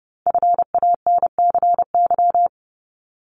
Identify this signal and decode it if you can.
Morse code, heard as a keyed tone, about 30 wpm, 700 Hz. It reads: FANCY
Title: Multi-letter Phonemes 30wpm